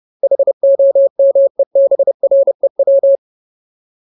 Morse code, heard as a keyed tone, about 30 wpm, 550 Hz. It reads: HOMEBREW
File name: Ultimate Word List / 30WPM Ultimate